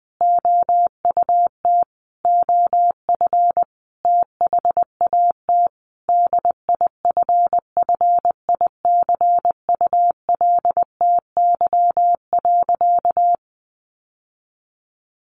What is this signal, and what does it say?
Morse code, heard as a keyed tone, about 20 wpm, 700 Hz. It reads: OUT OF THAT DIFFICULTY.
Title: Sentences from Top 1500 Words 20wpm